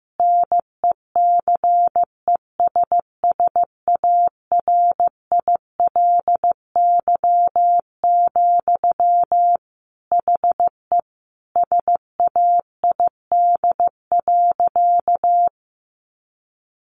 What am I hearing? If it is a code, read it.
Morse code, heard as a keyed tone, about 15 wpm, 700 Hz. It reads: NECESSARILY, HE SAID.
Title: Sentences from Top 1400 Words 15wpm